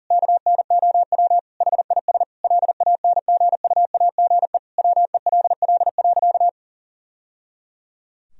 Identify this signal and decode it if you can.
Morse code, heard as a keyed tone, about 40 wpm, 700 Hz. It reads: KNOW HIS LANGUAGE WELL.
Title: Sentences from Top 1000 Words - Review Part 3 - 40wpm